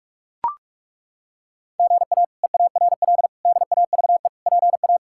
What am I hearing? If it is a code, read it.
Morse code, heard as a keyed tone, about 42 wpm, 700 Hz. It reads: GA EARL DAVE PA
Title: K1USN SST QSO Practice - 45wpm